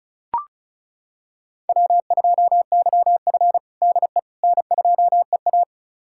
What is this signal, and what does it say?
Morse code, heard as a keyed tone, about 36 wpm, 700 Hz. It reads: W2YF DE N2EA
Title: USA Callsign DE Callsign - 4 Characters - 35wpm